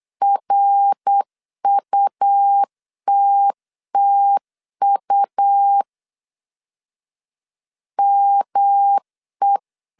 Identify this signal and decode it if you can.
Morse code, heard as a keyed tone, about 8 wpm, 800 Hz. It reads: RUTTU ME
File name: CW / DET / 1 3 1500C S C 8wpm 4214